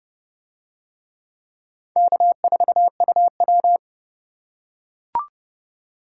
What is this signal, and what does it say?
Morse code, heard as a keyed tone, about 27 wpm, 700 Hz. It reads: K4UW